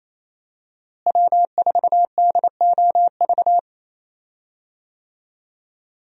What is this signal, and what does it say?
Morse code, heard as a keyed tone, about 28 wpm, 700 Hz. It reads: W4DOV
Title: USA Call Signs - 5 Characters - Repeated 3x - 28wpm